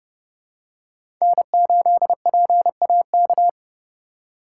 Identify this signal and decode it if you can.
Morse code, heard as a keyed tone, about 30 wpm, 700 Hz. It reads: N8PAK
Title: USA Call Signs - 5 Characters - 4x Character Spacing - 30wpm